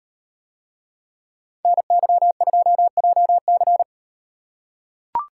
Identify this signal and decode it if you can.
Morse code, heard as a keyed tone, about 36 wpm, 700 Hz. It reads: NY2JC